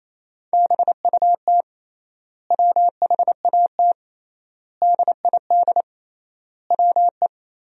Morse code, heard as a keyed tone, about 28 wpm, 700 Hz. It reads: BUT WHAT DID WE
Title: 3x Word-Spacing - Sentences from Top 400 Words - 28wpm